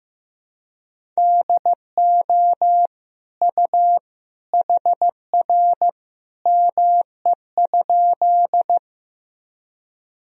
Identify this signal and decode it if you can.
Morse code, heard as a keyed tone, about 16 wpm, 700 Hz. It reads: DO U HR ME?
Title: Sentences from Top 300 Words and Abbreviations 15wpm